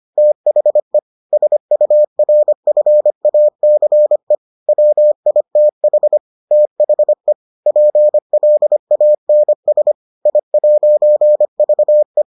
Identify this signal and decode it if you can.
Morse code, heard as a keyed tone, about 25 wpm, 600 Hz. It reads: THE SURFACE WITH THE PLANS I'VE